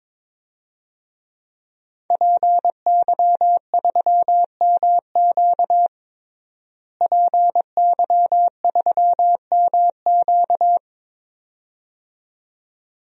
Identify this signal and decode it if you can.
Morse code, heard as a keyed tone, about 21 wpm, 700 Hz. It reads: PY3MQ PY3MQ
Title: DXCC-100 Call Signs - 5 Characters - Repeated 2x - 22wpm